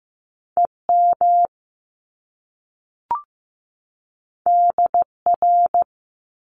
Morse code, heard as a keyed tone, about 15 wpm, 700 Hz. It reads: EM DR
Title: Top Two Letter Combinations in Words 15wpm